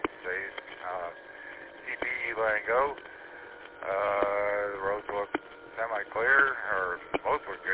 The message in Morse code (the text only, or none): none